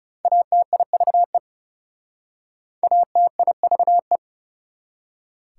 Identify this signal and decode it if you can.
Morse code, heard as a keyed tone, about 40 wpm, 690 Hz. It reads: ATIVE ATIVE